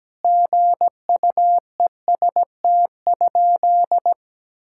Morse code, heard as a keyed tone, about 17 wpm, 700 Hz. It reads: GUEST?
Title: Sentences from Top 1600 Words 17wpm